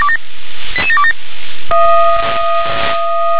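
Five-tone selective call (ZVEI), heard as quick runs of five tones, tones of about 70 ms. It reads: 88727
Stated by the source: ZVEI five-tone call received by SDR.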